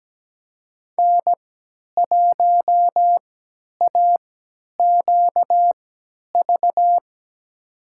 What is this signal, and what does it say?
Morse code, heard as a keyed tone, about 17 wpm, spaced as for 10 wpm, 700 Hz. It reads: N1AQV
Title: USA Call Signs - 5 Characters - 3x Character Spacing - 17wpm